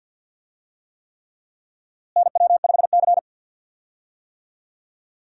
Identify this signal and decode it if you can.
Morse code, heard as a keyed tone, about 52 wpm, 700 Hz. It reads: NW5C
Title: USA Call Signs - 4 Characters - Repeated 3x - 50wpm